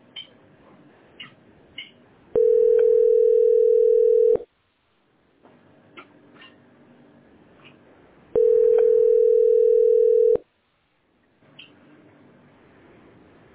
Ringback tone (North America), 440 + 480 Hz.